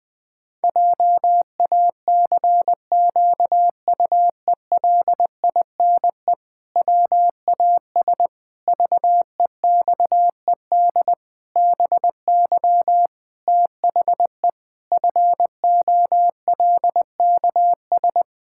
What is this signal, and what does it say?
Morse code, heard as a keyed tone, about 20 wpm, 700 Hz. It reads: JACQUELINE WAS VEXED BY THE FOLKS
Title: Pangram Challenge v2 - 20wpm